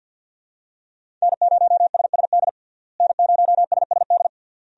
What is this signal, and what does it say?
Morse code, heard as a keyed tone, about 47 wpm, 700 Hz. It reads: N0SSD N0SSD
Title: USA Call Signs - 5 Characters - Repeated 3x - 50wpm